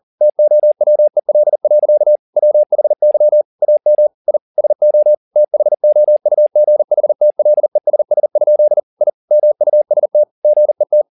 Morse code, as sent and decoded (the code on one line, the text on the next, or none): - --- .-- . .-.. .-.-.- / .-- .... -.-- / .- -- / .. / ... --- / - .... --- ..- --. .... - .-.. ... ... ..--.. / .. / -- ..- ... - / --. -
TOWEL. WHY AM I SO THOUGHTLESS? I MUST GET